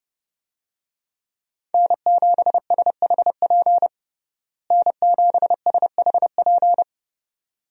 Morse code, heard as a keyed tone, about 30 wpm, 700 Hz. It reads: N7SHP N7SHP
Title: USA Call Signs - 5 Characters - Repeated 2x - 30wpm